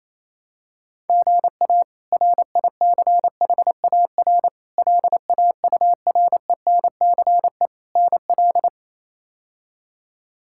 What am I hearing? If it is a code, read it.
Morse code, heard as a keyed tone, about 28 wpm, 700 Hz. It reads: GA RICHAR LAURENCE NL
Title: K1USN SST QSO Practice - 28wpm